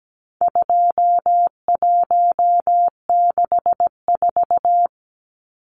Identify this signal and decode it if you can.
Morse code, heard as a keyed tone, about 17 wpm, 700 Hz. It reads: 2164
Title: Numbers - 4-Digit 17wpm